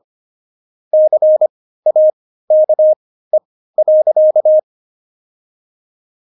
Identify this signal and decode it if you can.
Morse code, heard as a keyed tone, about 25 wpm, spaced as for 15 wpm, 620 Hz. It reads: CAKE.
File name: Jokes-25-char-15-eff